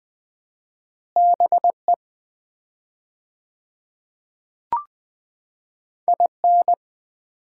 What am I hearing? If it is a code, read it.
Morse code, heard as a keyed tone, about 20 wpm, 700 Hz. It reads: BE IN
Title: 4x Word-Spacing - Sets of 5 Words - 20wpm